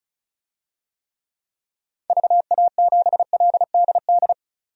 Morse code, heard as a keyed tone, about 34 wpm, 700 Hz. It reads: VA7LDD
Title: DXCC-100 Call Signs - 6 Characters - 35wpm